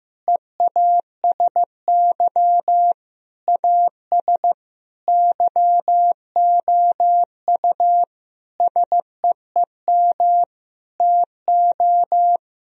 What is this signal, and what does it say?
Morse code, heard as a keyed tone, about 15 wpm, 700 Hz. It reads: EASY AS YOU SEEM TO
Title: Sentences from Top 1000 Words - Review Part 1 - 15wpm